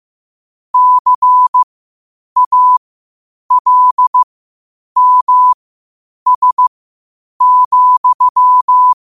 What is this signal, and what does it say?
Morse code, heard as a keyed tone, about 15 wpm, spaced as for 8 wpm, 1000 Hz. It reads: CALMS,